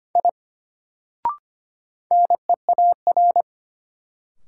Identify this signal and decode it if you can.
Morse code, heard as a keyed tone, about 25 wpm, 700 Hz. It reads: I NEAR
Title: Sentences from Top 200 Words - New Word Drill - 25wpm